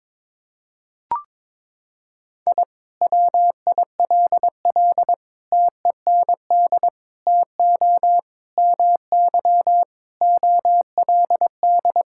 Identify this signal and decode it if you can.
Morse code, heard as a keyed tone, about 22 wpm, 700 Hz. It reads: I WILL TEND TO MY OLD
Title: Sentences from Top 1000 Words - Review Part 5 - 22wpm